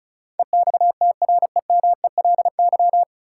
Morse code, heard as a keyed tone, about 36 wpm, 700 Hz. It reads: EXTREMELY
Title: Sentences from Top 1200 Words - New Word Drill Alt - 35wpm